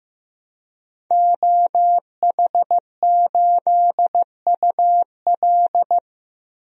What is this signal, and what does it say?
Morse code, heard as a keyed tone, about 15 wpm, 700 Hz. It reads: OH8UL
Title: DXCC-100 Call Signs - 5 Characters - Repeated 2x - 15wpm